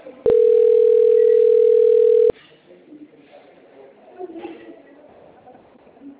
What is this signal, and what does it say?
Ringback tone (North America), 440 + 480 Hz.